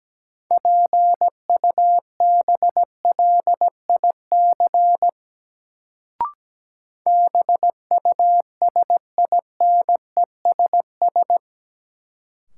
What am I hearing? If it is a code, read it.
Morse code, heard as a keyed tone, about 17 wpm, 700 Hz. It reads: PUBLIC BUSINESS
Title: Sentences from Top 600 Words - New Word Drill - 17wpm